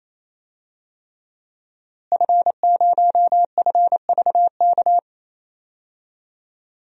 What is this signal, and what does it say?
Morse code, heard as a keyed tone, about 27 wpm, 700 Hz. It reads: F0FVK
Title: DXCC-100 Call Signs - 5 Characters - Repeated 3x - 28wpm